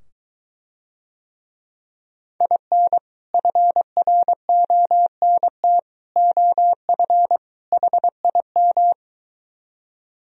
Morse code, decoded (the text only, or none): IN FRONT OF HIM